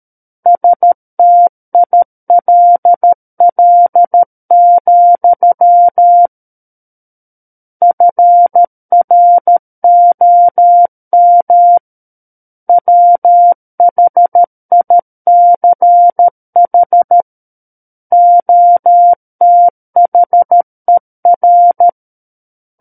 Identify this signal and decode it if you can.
Morse code, heard as a keyed tone, about 13 wpm, 700 Hz. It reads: STILL, FROM WHICH OTHER
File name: Book One / 04-Chapter 4 - 13 WPM